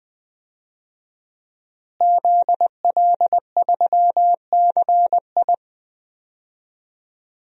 Morse code, heard as a keyed tone, about 20 wpm, 700 Hz. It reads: ZL3CI
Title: DXCC-100 Call Signs - 5 Characters - Repeated 2x - 20wpm